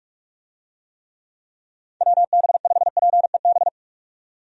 Morse code, heard as a keyed tone, about 47 wpm, 700 Hz. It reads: WB5PEB
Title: USA Call Signs - 6 Characters - 45wpm